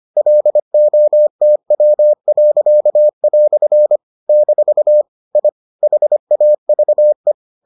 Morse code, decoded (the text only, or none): LOTW." - I HAVE